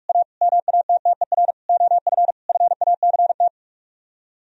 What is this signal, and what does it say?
Morse code, heard as a keyed tone, about 49 wpm, 700 Hz. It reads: A MATTER OF FACT